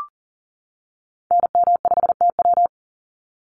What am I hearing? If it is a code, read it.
Morse code, heard as a keyed tone, about 40 wpm, 700 Hz. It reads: NM5TW
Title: NA SOTA Activator Callsigns 40wpm